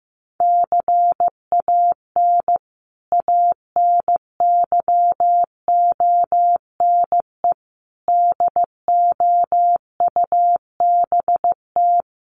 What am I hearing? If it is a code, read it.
Morse code, heard as a keyed tone, about 15 wpm, 700 Hz. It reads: CAN ANYONE DOUBT
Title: Sentences from Top 1500 Words 15wpm